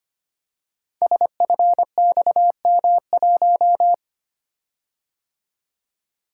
Morse code, heard as a keyed tone, about 25 wpm, 700 Hz. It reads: SFXM1
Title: Code Groups - 5 Letter:Numbers - 25wpm